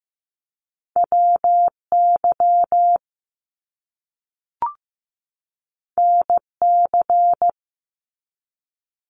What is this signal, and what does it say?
Morse code, heard as a keyed tone, about 15 wpm, 700 Hz. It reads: WY NC
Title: US States 15wpm